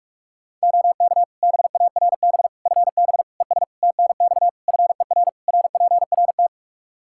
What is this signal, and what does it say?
Morse code, heard as a keyed tone, about 41 wpm, 700 Hz. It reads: OK BARB FB ES TNX FER RPRT
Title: Practice QSO - Exchange 2 - Info - 45wpm